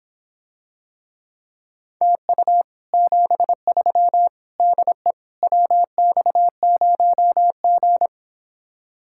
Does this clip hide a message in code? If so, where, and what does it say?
Yes: Morse code from 2.0 to 8.1 s, heard as a keyed tone, about 26 wpm, 700 Hz. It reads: TU 73 DE WX0G